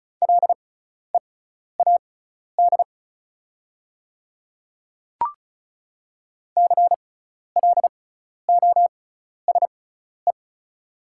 Morse code, heard as a keyed tone, about 36 wpm, spaced as for 12 wpm, 700 Hz. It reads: LEAD CLOSE
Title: 6x Character Spacing - Top 500 Words - 35wpm